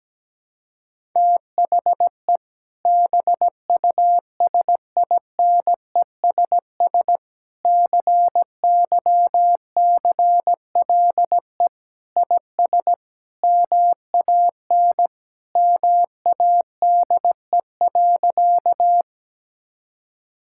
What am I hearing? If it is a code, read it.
Morse code, heard as a keyed tone, about 17 wpm, 700 Hz. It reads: THE BUSINESS CYCLE IS MAN MADE.
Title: Sentences from Top 1000 Words - Review Part 6 - 17wpm